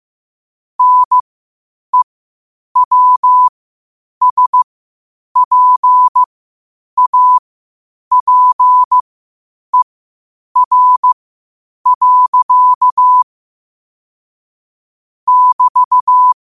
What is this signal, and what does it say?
Morse code, heard as a keyed tone, about 15 wpm, spaced as for 8 wpm, 1000 Hz. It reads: NEWSPAPER. <BT>
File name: qotdfiles / 8wpm